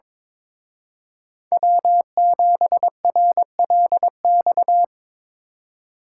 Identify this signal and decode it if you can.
Morse code, heard as a keyed tone, about 26 wpm, 700 Hz. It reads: W7RLX
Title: Speed Racing - USA Call Signs 22wpm